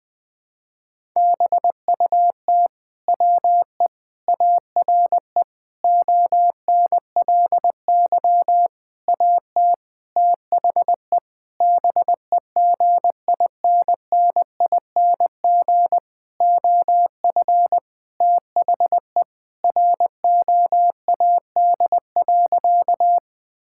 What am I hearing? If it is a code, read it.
Morse code, heard as a keyed tone, about 21 wpm, 700 Hz. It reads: BUT WE ARE ONLY AT THE BEGINNING OF THE ROAD.
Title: Sentences from Top 1000 Words - Review Part 5 - 20wpm